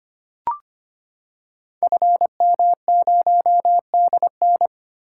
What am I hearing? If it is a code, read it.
Morse code, heard as a keyed tone, about 25 wpm, 700 Hz. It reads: FM0DN
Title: Code Groups - 5 Letter:Numbers - 25wpm